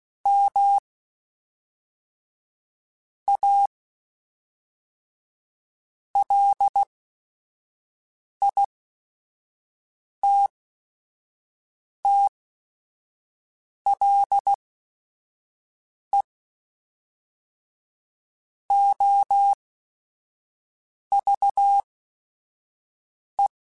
Morse code, heard as a keyed tone, about 16 wpm, spaced as for 5 wpm, 800 Hz. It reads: M A LITTLE OVE